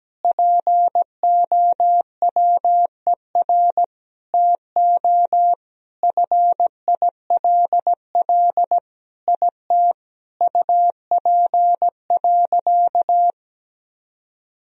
Morse code, heard as a keyed tone, about 17 wpm, 700 Hz. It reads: POWER TO FILL IT UP.
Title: Sentences from Top 1000 Words - Review Part 3 - 17wpm